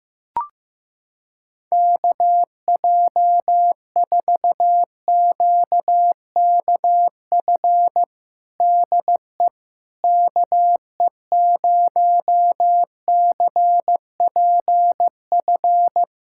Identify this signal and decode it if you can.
Morse code, heard as a keyed tone, about 15 wpm, 700 Hz. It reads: KJ4QKF DE KE0CPF
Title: USA Callsign DE Callsign - 6 Characters - 15wpm